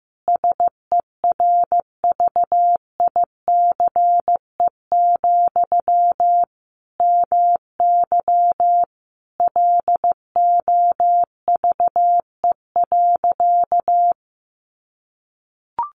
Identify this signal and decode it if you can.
Morse code, heard as a keyed tone, about 15 wpm, 700 Hz. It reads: SERVICE, MY LOVE.
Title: Sentences from Top 1100 Words 15wpm